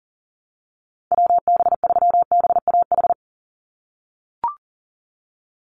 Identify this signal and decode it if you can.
Morse code, heard as a keyed tone, about 40 wpm, 700 Hz. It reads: WB3BAH
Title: USA Call Signs 40wpm